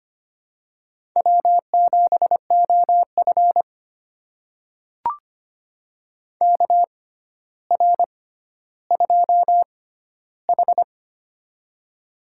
Morse code, heard as a keyed tone, about 25 wpm, spaced as for 13 wpm, 700 Hz. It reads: W7OF KR2H
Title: USA Call Signs - 4 Characters - 6x Character Spacing - 25wpm